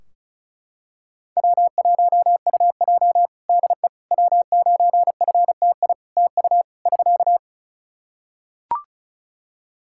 Morse code, heard as a keyed tone, about 34 wpm, 700 Hz. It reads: W1UJ DE W9FTI TU <SK>
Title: Practice QSO - Exchange 4 - Ending - 35wpm